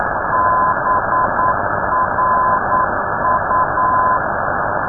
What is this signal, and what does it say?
Morse code, heard as a keyed tone, about 9 wpm, 1020 Hz. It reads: BRU
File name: europe / d-3079-cw-id-2005-rb